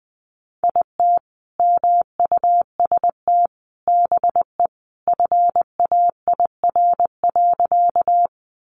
Morse code, heard as a keyed tone, about 20 wpm, 700 Hz. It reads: IT MUST BE FAIR.
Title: Sentences from Top 1300 Words 20wpm